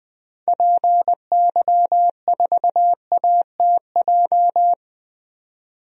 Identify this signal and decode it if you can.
Morse code, heard as a keyed tone, about 20 wpm, 700 Hz. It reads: PY4ATJ
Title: DXCC-100 Call Signs - 6 Characters - 20wpm